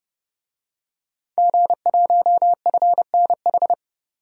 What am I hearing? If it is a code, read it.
Morse code, heard as a keyed tone, about 30 wpm, 700 Hz. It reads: G1FNH
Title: DXCC-100 Call Signs - 5 Characters - 30wpm